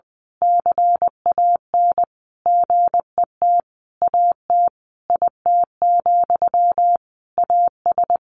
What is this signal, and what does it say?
Morse code, heard as a keyed tone, about 20 wpm, 700 Hz. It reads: CAN GET AT IT, AS
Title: A Sound Of Thunder 20wpm